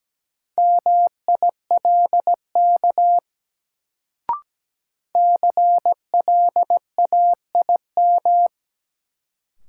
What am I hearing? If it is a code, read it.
Morse code, heard as a keyed tone, about 17 wpm, 700 Hz. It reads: MILK CLAIM
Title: Sentences from Top 1500 Words - New Word Drill - 17wpm